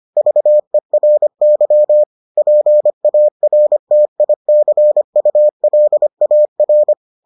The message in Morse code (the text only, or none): VERY PARTICULAR